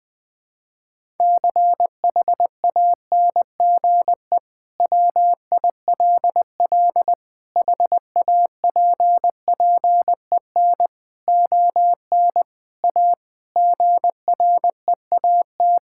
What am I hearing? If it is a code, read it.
Morse code, heard as a keyed tone, about 20 wpm, 700 Hz. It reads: CHANGE WILL HAPPEN ON A GREAT
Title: Sentences from Top 1000 Words - Review Part 5 - 20wpm